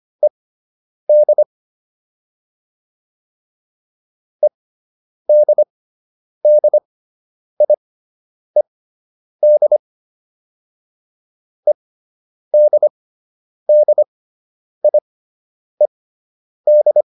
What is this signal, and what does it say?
Morse code, heard as a keyed tone, about 25 wpm, spaced as for 9 wpm, 600 Hz. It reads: ED EDDIED EDDIED